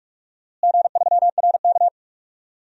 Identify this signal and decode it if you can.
Morse code, heard as a keyed tone, about 46 wpm, 700 Hz. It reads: G3RK
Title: DXCC-100 Call Signs - 4 Characters - Repeated 2x - 45wpm